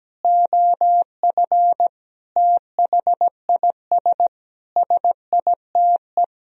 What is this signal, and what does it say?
Morse code, heard as a keyed tone, about 17 wpm, 700 Hz. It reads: OF THIS SITE
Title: Sentences from Top 1000 Words - Review Part 1 - 17wpm